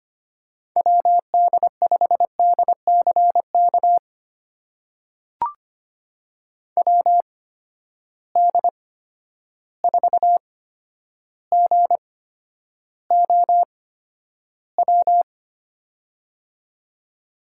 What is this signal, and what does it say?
Morse code, heard as a keyed tone, about 25 wpm, spaced as for 11 wpm, 700 Hz. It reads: WD5DCK WD4GOW
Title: USA Call Signs - 6 Characters - 8x Character Spacing - 25wpm